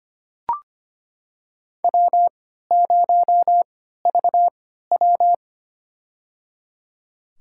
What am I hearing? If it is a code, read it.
Morse code, heard as a keyed tone, about 25 wpm, spaced as for 14 wpm, 700 Hz. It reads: W0VW
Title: USA Call Signs - 4 Characters - 3x Character Spacing - 25wpm